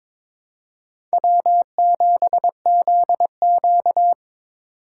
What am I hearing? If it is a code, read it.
Morse code, heard as a keyed tone, about 22 wpm, 700 Hz. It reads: W7ZQ
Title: USA Call Signs - 4 Characters - 4x Character Spacing - 22wpm